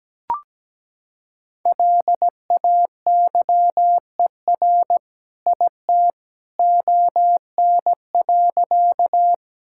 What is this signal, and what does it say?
Morse code, heard as a keyed tone, about 17 wpm, 700 Hz. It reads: LAYER IT ON.
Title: Sentences from Top 1600 Words 17wpm